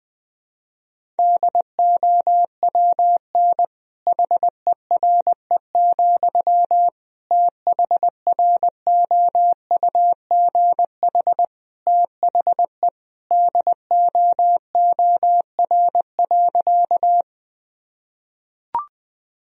Morse code, heard as a keyed tone, about 20 wpm, 700 Hz. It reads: DOWN HERE, THROUGH THE DOOR.
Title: Sentences from Top 1000 Words - Review Part 1 - 20wpm